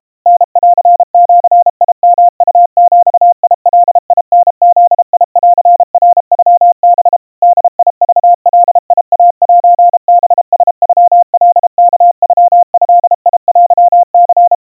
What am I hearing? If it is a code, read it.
Morse code (Japanese Wabun code), heard as a keyed tone, about 32 wpm, 700 Hz. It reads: タンジヨウビガタブンナノハ ボクガイセハラノカワノドテニ
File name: kawa / Morse sample / kamyu 120 cpm 700 Hz Crispy